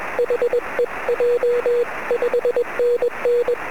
Morse code, heard as a keyed tone, about 21 wpm, 480 Hz, with noise. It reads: HEJ 5NN
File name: sm7lcb / 19-sm7gvf 23cm